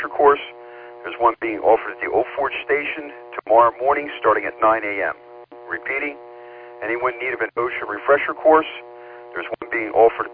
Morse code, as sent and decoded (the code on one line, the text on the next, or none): none